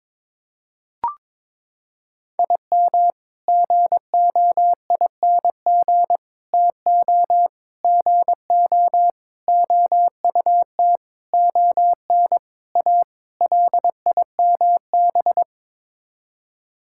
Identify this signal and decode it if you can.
Morse code, heard as a keyed tone, about 22 wpm, 700 Hz. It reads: IM GOING TO GO OUT ON A LIMB
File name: practice / 22wpm / Sets of 8 Words - 22wpm